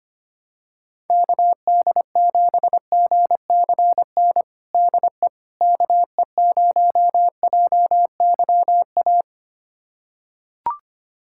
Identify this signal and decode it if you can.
Morse code, heard as a keyed tone, about 25 wpm, 700 Hz. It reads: KD7GCN DE KE0JYA
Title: USA Callsign DE Callsign 25wpm